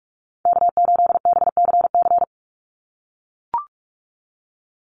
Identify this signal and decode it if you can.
Morse code, heard as a keyed tone, about 44 wpm, 700 Hz. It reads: K8BCC
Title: USA Call Signs 45wpm